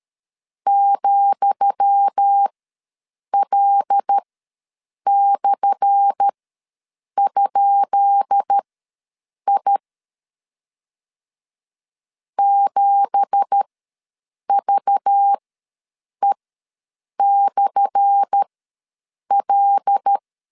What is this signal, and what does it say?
Morse code, heard as a keyed tone, about 13 wpm, spaced as for 7 wpm, 800 Hz. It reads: ,L/?I 7VE/L
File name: LEZIONI / 10 SIGNS / 10D COMA ws 12wpm 3618